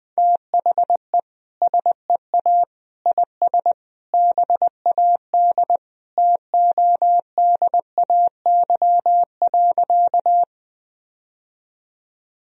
Morse code, heard as a keyed tone, about 20 wpm, 700 Hz. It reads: THE SEA IS BAD TODAY.
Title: Sentences from Top 1000 Words - Review Part 1 - 20wpm